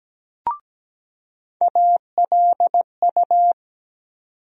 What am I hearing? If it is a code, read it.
Morse code, heard as a keyed tone, about 17 wpm, 700 Hz. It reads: ALU
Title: USA Call Sign Postfixes - 3 Letters 17wpm